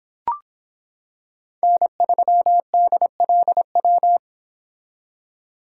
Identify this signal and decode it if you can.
Morse code, heard as a keyed tone, about 25 wpm, 700 Hz. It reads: N3DLW